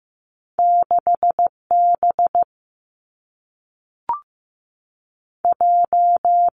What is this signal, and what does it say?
Morse code, heard as a keyed tone, about 15 wpm, 700 Hz. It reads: 6 B J
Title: Two Challenging Letter-Number Combinations 15wpm